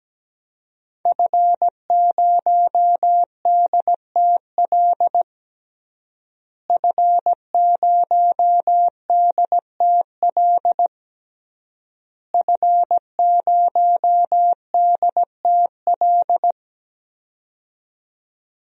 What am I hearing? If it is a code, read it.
Morse code, heard as a keyed tone, about 17 wpm, 700 Hz. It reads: F0DTL F0DTL F0DTL
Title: DXCC-100 Call Signs - 5 Characters - Repeated 3x - 17wpm